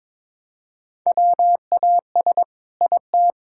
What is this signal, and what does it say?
Morse code, heard as a keyed tone, about 21 wpm, 700 Hz. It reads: WAS IT
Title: Sentences from Top 1600 Words 22wpm